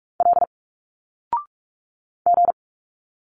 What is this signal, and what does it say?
Morse code, heard as a keyed tone, about 46 wpm, 700 Hz. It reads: L G
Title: Single Letters 45wpm